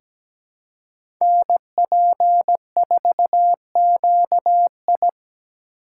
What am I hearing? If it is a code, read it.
Morse code, heard as a keyed tone, about 17 wpm, 700 Hz. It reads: NP4QI
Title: USA Call Signs - 5 Characters - 2x Character Spacing - 17wpm